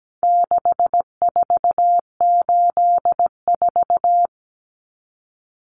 Morse code, heard as a keyed tone, about 17 wpm, 700 Hz. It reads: 6484
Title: Numbers - 4-Digit 17wpm